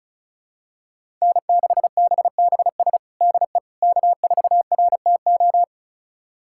Morse code, heard as a keyed tone, about 36 wpm, 700 Hz. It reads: N6BBS DE K4RTO
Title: USA Callsign DE Callsign - 5 Characters - 35wpm